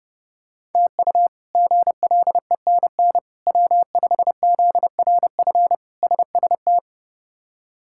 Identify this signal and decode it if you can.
Morse code, heard as a keyed tone, about 31 wpm, 700 Hz. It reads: TU GLENN W5ZRF SST
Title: K1USN SST QSO Practice - 30wpm